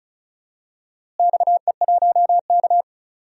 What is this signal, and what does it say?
Morse code, heard as a keyed tone, about 36 wpm, 700 Hz. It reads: XE1K
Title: DXCC-100 Call Signs - 4 Characters - Repeated 2x - 35wpm